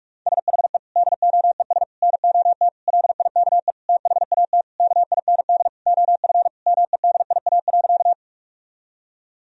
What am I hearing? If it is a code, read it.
Morse code, heard as a keyed tone, about 48 wpm, 700 Hz. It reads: SHE DOES NOT LIKE THAT KIND OF MEDIA.
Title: Sentences from Top 1000 Words - Review Part 4 - 45wpm